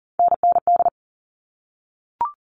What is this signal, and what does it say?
Morse code, heard as a keyed tone, about 40 wpm, 700 Hz. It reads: NND
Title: All Cut Numbers - 3-Digit 40wpm